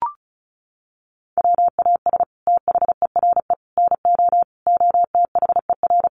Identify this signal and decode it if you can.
Morse code, heard as a keyed tone, about 34 wpm, 700 Hz. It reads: WAS THERE NO OTHER
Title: Sentences from Top 1400 Words 35wpm